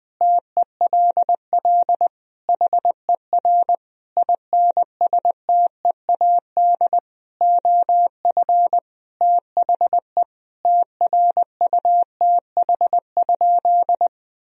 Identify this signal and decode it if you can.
Morse code, heard as a keyed tone, about 21 wpm, 700 Hz. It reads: TELL HER INSTEAD OF THE TRUTH?
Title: Sentences from Top 1000 Words - Review Part 4 - 20wpm